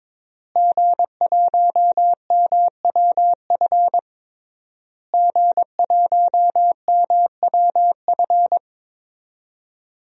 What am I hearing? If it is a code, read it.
Morse code, heard as a keyed tone, about 22 wpm, 700 Hz. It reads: G1MWF G1MWF
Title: DXCC-100 Call Signs - 5 Characters - Repeated 3x - 22wpm